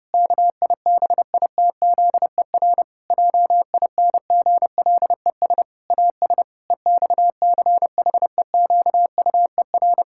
Morse code, heard as a keyed tone, about 30 wpm, 700 Hz. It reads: KIBITZER JINGLES AS EXCHEQUER
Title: Pangram Challenge v2 - 30wpm